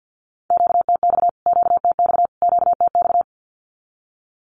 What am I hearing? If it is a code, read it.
Morse code, heard as a keyed tone, about 56 wpm, 700 Hz. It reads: QTX QTX QTX
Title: Q-Codes Repeated 3x 50wpm